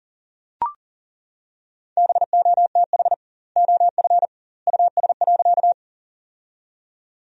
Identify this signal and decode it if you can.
Morse code, heard as a keyed tone, about 40 wpm, 700 Hz. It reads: BOTH OF US.
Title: 2x Word-Spacing - Sentences from Top 300 Words - 40wpm